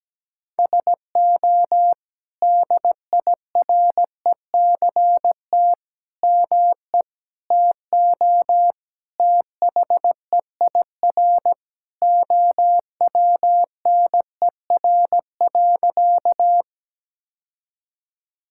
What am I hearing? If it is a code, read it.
Morse code, heard as a keyed tone, about 17 wpm, 700 Hz. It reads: SO DIRECT ME TO THEIR OWNER.
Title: Sentences from Top 1600 Words 17wpm